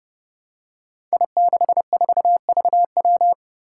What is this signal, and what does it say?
Morse code, heard as a keyed tone, about 31 wpm, 700 Hz. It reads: I64VW
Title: Five Challenging Letter-Number Combinations 30wpm